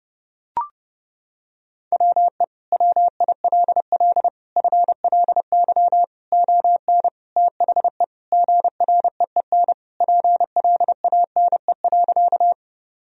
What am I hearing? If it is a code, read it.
Morse code, heard as a keyed tone, about 30 wpm, 700 Hz. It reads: WE WILL FLY ON THE GREEN PLANE.
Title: Sentences from Top 1000 Words - Review Part 3 - 30wpm